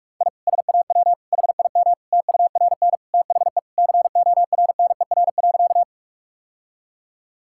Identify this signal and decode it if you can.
Morse code, heard as a keyed tone, about 43 wpm, 700 Hz. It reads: I SAW HIM TURN THE CORNER.
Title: Sentences from Top 1600 Words 45wpm